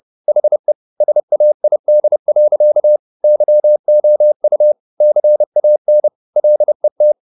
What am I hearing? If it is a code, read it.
Morse code, heard as a keyed tone, about 30 wpm, 600 Hz. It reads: HE SAID. YOU CAN LET
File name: the old man and the sea / the old man and the sea 1 30wpm